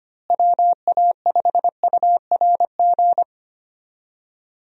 Morse code, heard as a keyed tone, about 25 wpm, 700 Hz. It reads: WA5URG
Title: USA Call Signs - 6 Characters - Repeated 2x - 25wpm